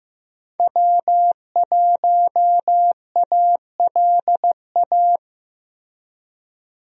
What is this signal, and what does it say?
Morse code, heard as a keyed tone, about 15 wpm, 700 Hz. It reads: W1ALA
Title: USA Call Signs - 5 Characters - Repeated 3x - 15wpm